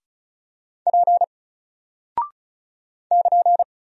Morse code, heard as a keyed tone, about 36 wpm, 700 Hz. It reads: P <KN>
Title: Single Character (Letter:Number:Punctuation:Prosign) Alt 35wpm